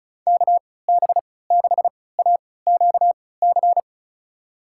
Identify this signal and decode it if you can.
Morse code, heard as a keyed tone, about 36 wpm, spaced as for 20 wpm, 700 Hz. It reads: KB6AQC
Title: USA Call Signs - 6 Characters - 3x Character Spacing - 35wpm